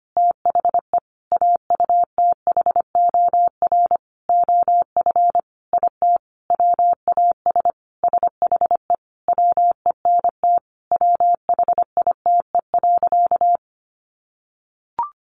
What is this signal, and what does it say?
Morse code, heard as a keyed tone, about 25 wpm, 700 Hz. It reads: THE AUTHOR OF IT WAS SHE WENT WHITE.
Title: Sentences from Top 1200 Words 25wpm